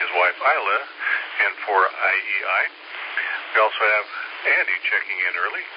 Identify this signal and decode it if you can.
Emergency Alert System (SAME) tones (header bursts), heard as harsh buzzing bursts, with speech. It is heard from 0.9 to 5.4 s.